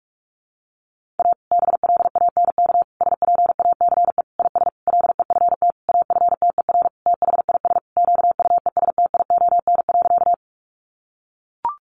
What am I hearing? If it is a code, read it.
Morse code, heard as a keyed tone, about 44 wpm, 700 Hz. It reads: A BLANK SPACE IS LEFT AFTER THIS QUESTION.
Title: Sentences from Top 1500 Words 45wpm